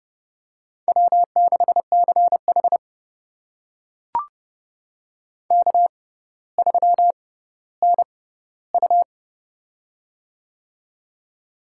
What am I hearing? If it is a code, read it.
Morse code, heard as a keyed tone, about 30 wpm, spaced as for 15 wpm, 700 Hz. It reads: W6CH K3NU
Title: USA Call Signs - 4 Characters - 6x Character Spacing - 30wpm